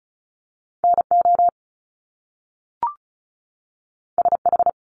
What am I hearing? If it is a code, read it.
Morse code, heard as a keyed tone, about 35 wpm, 700 Hz. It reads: NO SH
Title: Top Two Letter Combinations in Words 35wpm